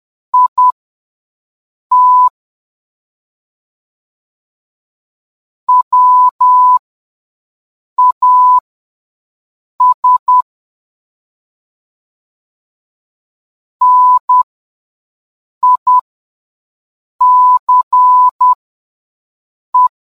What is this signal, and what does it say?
Morse code, heard as a keyed tone, about 10 wpm, spaced as for 5 wpm, 1000 Hz. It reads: IT WAS NICE